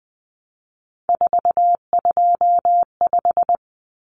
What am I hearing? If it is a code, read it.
Morse code, heard as a keyed tone, about 20 wpm, 700 Hz. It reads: 425
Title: Numbers - 3-Digit 20wpm